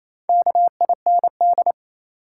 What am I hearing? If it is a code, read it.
Morse code, heard as a keyed tone, about 28 wpm, 700 Hz. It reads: KIND
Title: Sets of 2 Words - New Word Drill - 28wpm